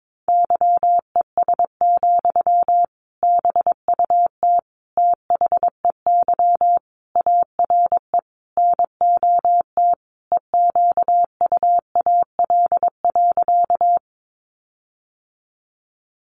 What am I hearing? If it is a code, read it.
Morse code, heard as a keyed tone, about 22 wpm, 700 Hz. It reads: YES, BUT THEY ARE NOT EQUAL.
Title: Sentences from Top 1400 Words 22wpm